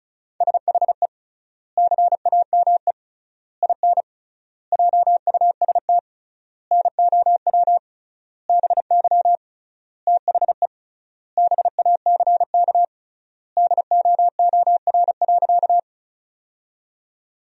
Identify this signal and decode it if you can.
Morse code, heard as a keyed tone, about 34 wpm, 700 Hz. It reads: SHE CAME IN JUST NOW BY THE BACK DOOR.
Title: 3x Word-Spacing - Sentences from Top 300 Words - 35wpm